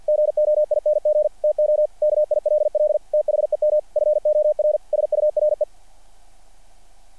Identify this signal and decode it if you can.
Morse code, heard as a keyed tone, about 50 wpm, 600 Hz. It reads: GOING TO KILL THEM FOR SURE
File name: archived / CRSnet